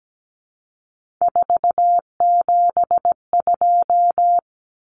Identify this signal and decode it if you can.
Morse code, heard as a keyed tone, about 17 wpm, 700 Hz. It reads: 472
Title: Numbers - 3-Digit 17wpm